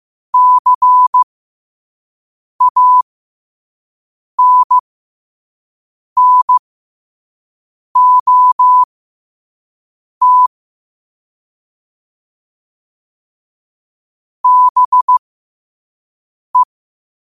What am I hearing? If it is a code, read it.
Morse code, heard as a keyed tone, about 15 wpm, spaced as for 5 wpm, 1000 Hz. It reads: CANNOT BE
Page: New quotes every day in morse code at 5 Words per minute.